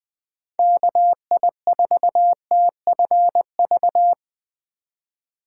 Morse code, heard as a keyed tone, about 21 wpm, 700 Hz. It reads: KI4TFV
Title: USA Call Signs - 4x Character Spacing - 20wpm